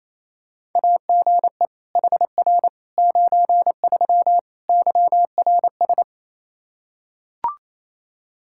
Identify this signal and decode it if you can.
Morse code, heard as a keyed tone, about 28 wpm, 700 Hz. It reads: AGE HR 93 YRS
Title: Practice QSO - Exchange 3 - More Info - 28wpm